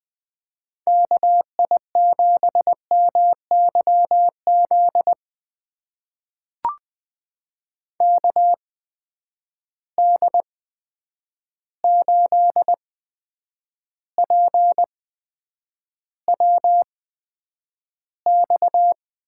Morse code, heard as a keyed tone, about 20 wpm, spaced as for 9 wpm, 700 Hz. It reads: KI7MYZ KD8PWX
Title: USA Call Signs - 8x Character Spacing - 20wpm